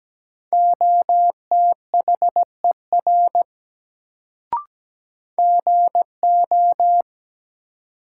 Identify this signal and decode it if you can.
Morse code, heard as a keyed tone, about 17 wpm, 700 Hz. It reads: OTHER GO
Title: Sentences from Top 100 Words - New Word Drill - 17wpm